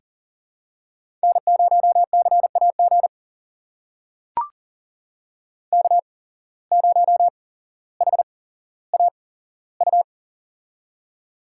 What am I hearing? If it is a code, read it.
Morse code, heard as a keyed tone, about 40 wpm, spaced as for 17 wpm, 700 Hz. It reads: N0CAG K0HAU
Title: USA Call Signs - 5 Characters - 8x Character Spacing - 40wpm